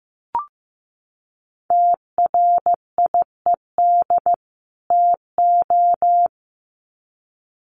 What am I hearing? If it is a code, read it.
Morse code, heard as a keyed tone, about 15 wpm, 700 Hz. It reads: TRIED TO
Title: Sets of 2 Words 15wpm